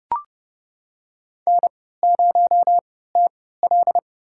Morse code, heard as a keyed tone, about 30 wpm, spaced as for 17 wpm, 700 Hz. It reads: N0TL
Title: USA Call Signs - 4 Characters - 3x Character Spacing - 30wpm